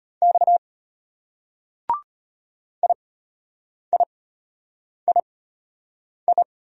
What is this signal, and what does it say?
Morse code, heard as a keyed tone, about 34 wpm, spaced as for 8 wpm, 700 Hz. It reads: X IIII